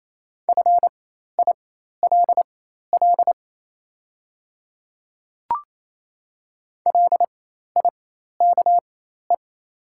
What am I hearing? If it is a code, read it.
Morse code, heard as a keyed tone, about 28 wpm, spaced as for 13 wpm, 700 Hz. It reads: FILL LIKE
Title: 4x Character Spacing - Top 500 Words - 28wpm